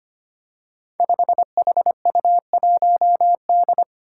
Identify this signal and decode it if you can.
Morse code, heard as a keyed tone, about 25 wpm, 700 Hz. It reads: 5HU1D
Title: Five Challenging Letter-Number Combinations 25wpm